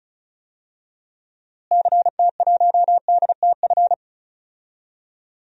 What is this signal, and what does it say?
Morse code, heard as a keyed tone, about 36 wpm, 700 Hz. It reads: CT1DTF
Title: DXCC-100 Call Signs - 6 Characters - Repeated 2x - 35wpm